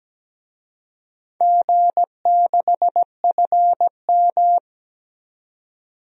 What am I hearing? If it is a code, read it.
Morse code, heard as a keyed tone, about 17 wpm, 700 Hz. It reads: G6FM
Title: DXCC-100 Call Signs - 4 Characters - 17wpm